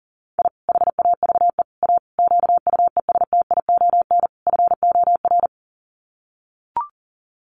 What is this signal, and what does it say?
Morse code, heard as a keyed tone, about 40 wpm, 700 Hz. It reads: I HAVE A QUESTION FOR
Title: Sets of 5 Words Encore2 40wpm